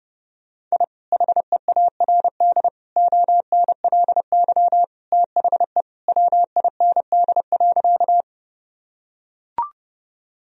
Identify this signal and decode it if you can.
Morse code, heard as a keyed tone, about 32 wpm, 700 Hz. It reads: I HEARD ONLY THE WIND.
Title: Sentences from Top 1000 Words - Review Part 2 - 30wpm